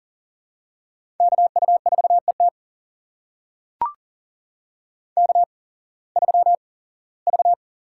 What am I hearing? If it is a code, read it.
Morse code, heard as a keyed tone, about 40 wpm, spaced as for 21 wpm, 700 Hz. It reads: KU4ET K3V
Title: USA Call Signs - 5 Characters - 8x Character Spacing - 40wpm